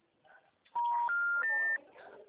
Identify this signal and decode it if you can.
Special information tone.